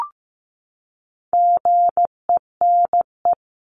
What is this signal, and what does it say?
Morse code, heard as a keyed tone, about 15 wpm, 700 Hz. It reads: GENE
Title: CWT Exchange 15wpm